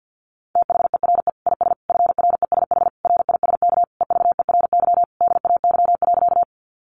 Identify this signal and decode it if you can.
Morse code, heard as a keyed tone, about 50 wpm, 700 Hz. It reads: THERE IS FRESH RISK EVERY DAY.
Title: Sentences from Top 1200 Words 50wpm